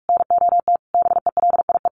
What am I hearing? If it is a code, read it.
Morse code, heard as a keyed tone, about 44 wpm, 700 Hz. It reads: NOT BELIE
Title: Sentences from Top 1000 Words 45wpm